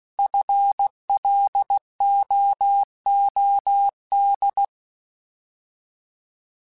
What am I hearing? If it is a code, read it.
Morse code, heard as a keyed tone, about 16 wpm, 800 Hz. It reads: FLOOD
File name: Alphabetical / Alphabetical mp3s / Fs @15